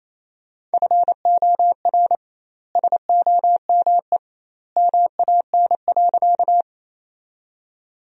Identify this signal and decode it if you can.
Morse code, heard as a keyed tone, about 28 wpm, 700 Hz. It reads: FOR SOME MAN.
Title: 2x Word-Spacing - Sentences from Top 200 Words - 28wpm